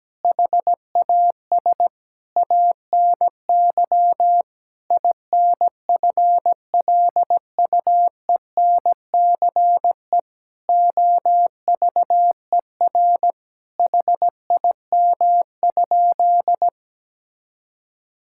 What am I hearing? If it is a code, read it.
Morse code, heard as a keyed tone, about 17 wpm, 700 Hz. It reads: HAS ANY INFLUENCE OVER HIM?
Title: Sentences from Top 1600 Words 17wpm